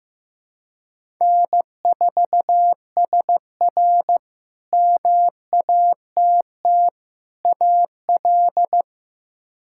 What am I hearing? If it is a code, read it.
Morse code, heard as a keyed tone, about 15 wpm, 700 Hz. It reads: N4SR MATT AL
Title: K1USN SST QSO Practice - 15wpm